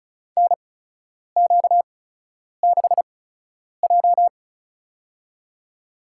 Morse code, heard as a keyed tone, about 34 wpm, spaced as for 10 wpm, 700 Hz. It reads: NQ6J
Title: USA Call Signs - 4 Characters - 8x Character Spacing - 35wpm